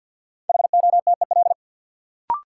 Morse code, heard as a keyed tone, about 48 wpm, 700 Hz. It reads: HOTEL
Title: Sentences from Top 1600 Words - New Word Drill - 50wpm